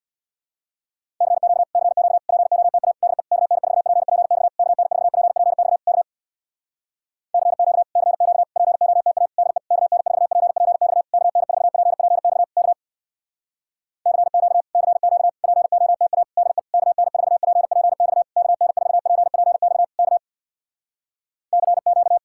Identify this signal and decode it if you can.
Morse code, heard as a keyed tone, about 64 wpm, 700 Hz. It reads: CQ CQ POTA DE KN4PPX KN4PPX K CQ CQ POTA DE KN4PPX KN4PPX K CQ CQ POTA DE KN4PPX KN4PPX K CQ